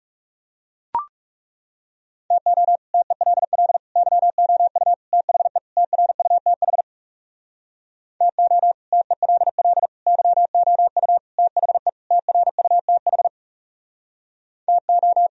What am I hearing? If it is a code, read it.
Morse code, heard as a keyed tone, about 40 wpm, 700 Hz. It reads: TO TELL YOU THE TRUTH TO TELL YOU THE TRUTH TO